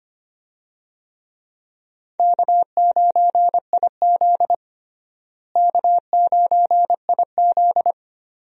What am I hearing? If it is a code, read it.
Morse code, heard as a keyed tone, about 25 wpm, 700 Hz. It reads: K9IZ K9IZ
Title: USA Call Signs - 4 Characters - Repeated 2x - 25wpm